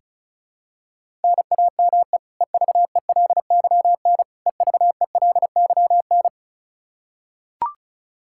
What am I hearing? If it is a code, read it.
Morse code, heard as a keyed tone, about 36 wpm, 700 Hz. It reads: NAME EVELYN EVELYN
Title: Practice QSO - Exchange 1 - Beginning - 35wpm